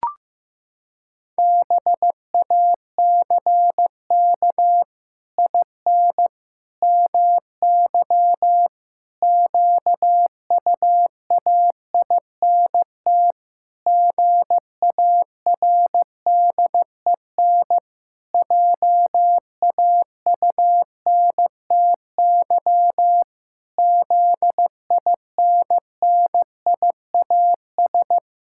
Morse code, heard as a keyed tone, about 15 wpm, 700 Hz. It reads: BACK IN MY QUAINT GARDEN JAUNTY ZINNIAS
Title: Pangram Challenge v2 - 15wpm